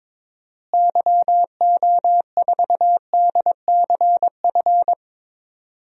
Morse code, heard as a keyed tone, about 22 wpm, 700 Hz. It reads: YO4DCF
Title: DXCC-100 Call Signs - 6 Characters - Repeated 2x - 22wpm